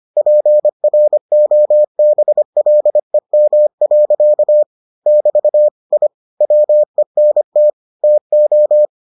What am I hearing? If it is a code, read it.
Morse code, heard as a keyed tone, about 25 wpm, 600 Hz. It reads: PROBLEM. <BT> I WENT TO